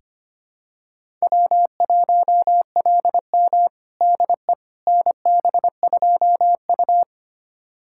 Morse code, heard as a keyed tone, about 25 wpm, 700 Hz. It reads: W1LM DE NB2U
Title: USA Callsign DE Callsign - 4 Characters - 25wpm